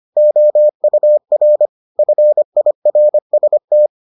Morse code, heard as a keyed tone, about 25 wpm, 600 Hz. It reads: OUR FIRST